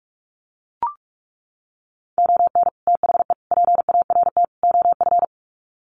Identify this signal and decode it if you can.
Morse code, heard as a keyed tone, about 44 wpm, 700 Hz. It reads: ON THE PART OF
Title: Sets of 4 Words 45wpm